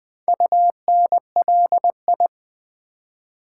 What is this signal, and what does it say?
Morse code, heard as a keyed tone, about 20 wpm, 690 Hz. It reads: UNLI
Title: Build A Word - 2 Letters At A Time - 6-Letter Words - 20wpm